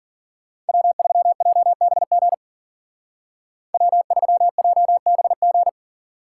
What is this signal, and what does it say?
Morse code, heard as a keyed tone, about 50 wpm, 700 Hz. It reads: W3JBG W3JBG
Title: Speed Racing - USA 1x3 Call Signs 40wpm